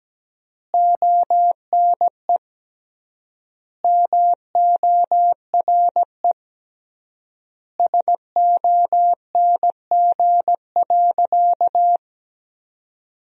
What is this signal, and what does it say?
Morse code, heard as a keyed tone, about 17 wpm, 700 Hz. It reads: ONE MORE SONG.
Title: 3x Word-Spacing - Sentences from Top 400 Words - 17wpm